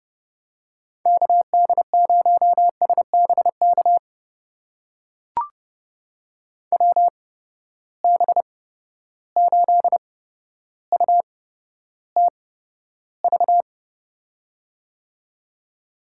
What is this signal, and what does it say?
Morse code, heard as a keyed tone, about 30 wpm, spaced as for 13 wpm, 700 Hz. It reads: KD0SBK WB8UTV
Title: USA Call Signs - 6 Characters - 8x Character Spacing - 30wpm